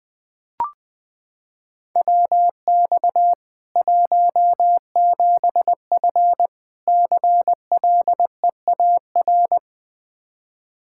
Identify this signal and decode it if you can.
Morse code, heard as a keyed tone, about 20 wpm, 700 Hz. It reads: WX 17F CLEAR
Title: Practice QSO - Exchange 2 - Info - 20wpm